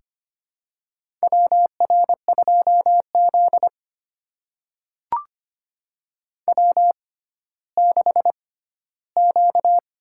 Morse code, heard as a keyed tone, about 25 wpm, spaced as for 14 wpm, 700 Hz. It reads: WR2Z W6Q
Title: USA Call Signs - 4 Characters - 6x Character Spacing - 25wpm